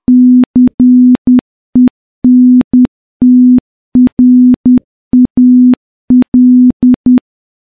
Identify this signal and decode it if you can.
Morse code, heard as a keyed tone, about 10 wpm, 250 Hz. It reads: CENTRAL
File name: morse-training-10WPM